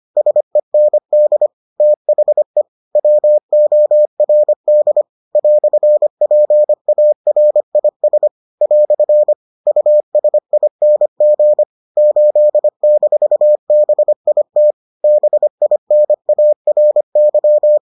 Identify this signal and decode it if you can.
Morse code, heard as a keyed tone, about 25 wpm, 600 Hz. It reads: SEND THE WORD "PARIS " USING 8-BIT BINARY